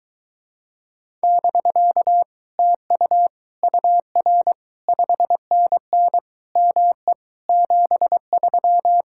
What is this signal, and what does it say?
Morse code, heard as a keyed tone, about 22 wpm, 700 Hz. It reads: <BK> TU UR 5NN ME 73